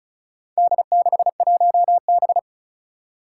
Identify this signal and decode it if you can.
Morse code, heard as a keyed tone, about 36 wpm, 700 Hz. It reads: D61B
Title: Four Challenging Letter-Number Combinations 35wpm